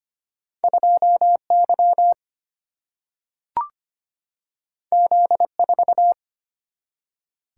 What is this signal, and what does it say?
Morse code, heard as a keyed tone, about 25 wpm, 700 Hz. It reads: 2 Y Z 4
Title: Single-Letter Single-Number 25wpm